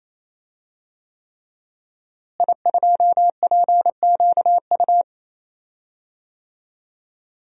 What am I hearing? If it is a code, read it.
Morse code, heard as a keyed tone, about 28 wpm, 700 Hz. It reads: I2PQU
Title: DXCC-100 Call Signs - 5 Characters - Repeated 3x - 28wpm